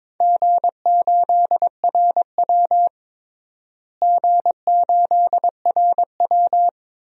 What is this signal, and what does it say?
Morse code, heard as a keyed tone, about 22 wpm, 700 Hz. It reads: G8RW G8RW
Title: DXCC-100 Call Signs - 4 Characters - Repeated 2x - 22wpm